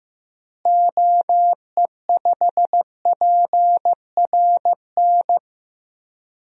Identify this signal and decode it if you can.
Morse code, heard as a keyed tone, about 15 wpm, 700 Hz. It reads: OE5PRN
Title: DXCC-100 Call Signs - 6 Characters - Repeated 3x - 15wpm